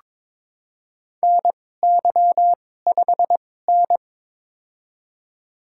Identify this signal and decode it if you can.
Morse code, heard as a keyed tone, about 22 wpm, spaced as for 16 wpm, 700 Hz. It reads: NY5N
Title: USA Call Signs - 4 Characters - 2x Character Spacing - 22wpm